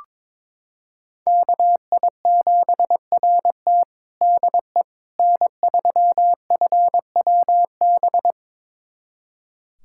Morse code, heard as a keyed tone, about 22 wpm, 700 Hz. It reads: KI7RT DE N3FWB
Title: USA Callsign DE Callsign - 5 Characters - 22wpm